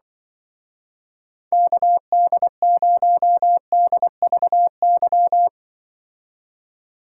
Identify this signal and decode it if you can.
Morse code, heard as a keyed tone, about 24 wpm, 700 Hz. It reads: KD0DVY
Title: Speed Racing - USA 2x3 Call Signs 20wpm